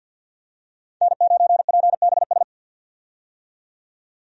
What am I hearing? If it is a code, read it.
Morse code, heard as a keyed tone, about 46 wpm, 700 Hz. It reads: N9PBS
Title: USA Call Signs - 5 Characters - 50wpm